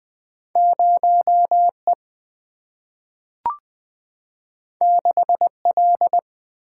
Morse code, heard as a keyed tone, about 20 wpm, 700 Hz. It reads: 0E 6L
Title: Single-Letter Single-Number 20wpm